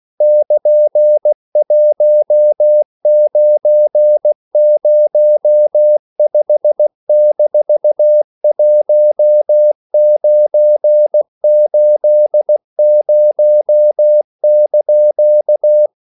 Morse code, heard as a keyed tone, about 16 wpm, 600 Hz. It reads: <KN>1905-1980)